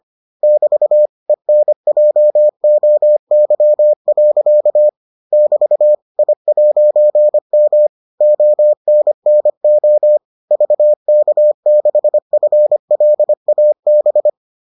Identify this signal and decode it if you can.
Morse code, heard as a keyed tone, about 25 wpm, 600 Hz. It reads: <BT> ENJOY. <BT> I'M ONNO VK6FLAB